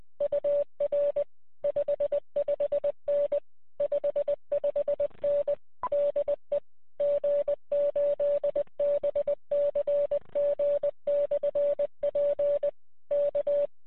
Morse code, heard as a keyed tone, about 20 wpm, 580 Hz. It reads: UR 55N 55N DE G8BCG/P K
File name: 9l1ab-qso